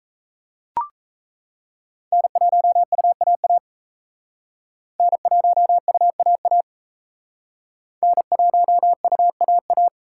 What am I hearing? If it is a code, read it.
Morse code, heard as a keyed tone, about 36 wpm, 700 Hz. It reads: N1UAA N1UAA N1UAA